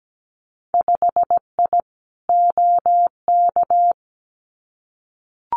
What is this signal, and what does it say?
Morse code, heard as a keyed tone, about 17 wpm, 700 Hz. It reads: 5I OK
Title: Winter Field Day Exchanges Encore 17wpm